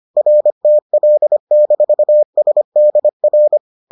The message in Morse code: RTL-SDR

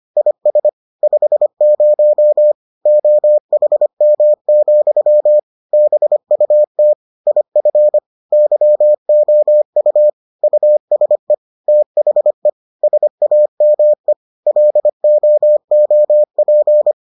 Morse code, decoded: IS 50 OHM, BUT IF YOU USE THE SAME LOOP

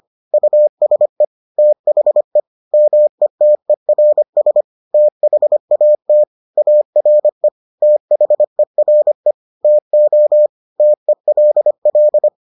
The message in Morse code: USE THE METERS THAT ARE THERE TO TELL

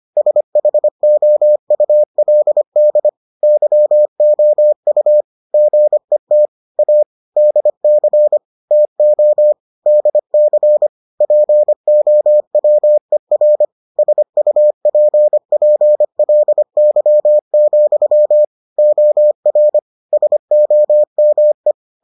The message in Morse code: SHOULD YOU GET A DC TO DC POWER SUPPLY, OR SOME

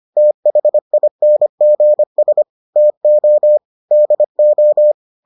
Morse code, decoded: THINGS TO DO

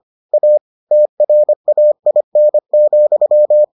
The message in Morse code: A TRAIN,

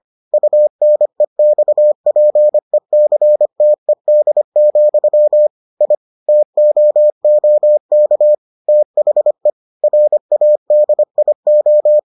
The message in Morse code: UNEXPECTED, I TOOK THE RADIO

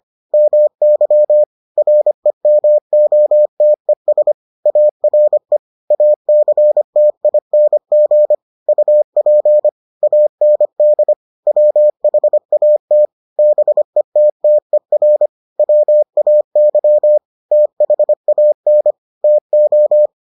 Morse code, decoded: MY REMOTES ARE ACTING UP AND WHAT BETTER WAY THAN TO

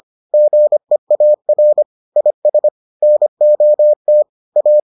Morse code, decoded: GEAR IS NOT A

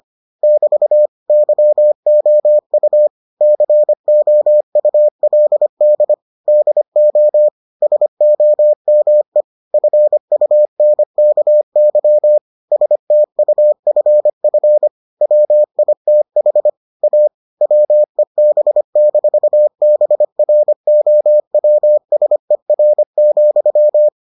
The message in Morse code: <BT> YOU COULD DO SOME FUNKY STUFF WITH A WEB-BROWSER,

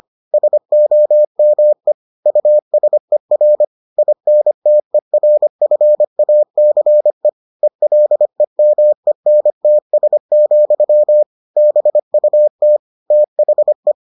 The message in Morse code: SOME USER INTERFACE ELEMENTS, BUT THE

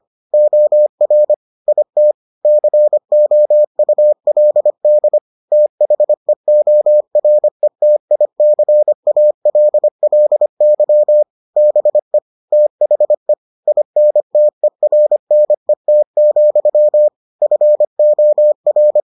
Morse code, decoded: OR IT COULD THEORETICALLY BE THE INTERNET, FOR

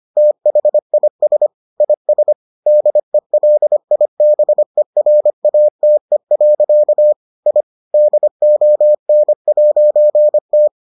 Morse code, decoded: THIS IS DELIBERATE. I DON'T

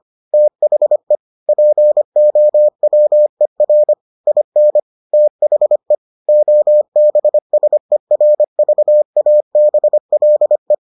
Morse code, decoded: THE POWER IN THE OBSERVABLE

